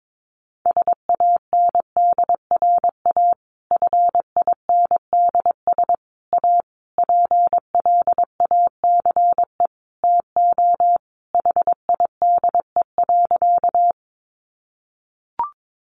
Morse code, heard as a keyed tone, about 22 wpm, 700 Hz. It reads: SANDRA FINDS A PLACE TO HIDE.
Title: Easy Phrases 22wpm